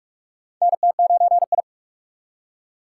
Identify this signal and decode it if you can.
Morse code, heard as a keyed tone, about 44 wpm, 700 Hz. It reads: NT9I
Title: USA Call Signs - 4 Characters - 6x Character Spacing - 45wpm